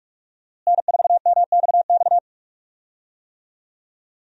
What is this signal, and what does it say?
Morse code, heard as a keyed tone, about 46 wpm, 700 Hz. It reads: N4MXX
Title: USA Call Signs - 5 Characters - Repeated 3x - 45wpm